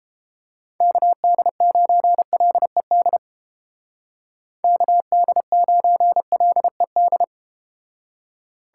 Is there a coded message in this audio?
Yes: Morse code from 0.8 to 7.2 s, heard as a keyed tone, about 32 wpm, 700 Hz. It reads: KD9LED KD9LED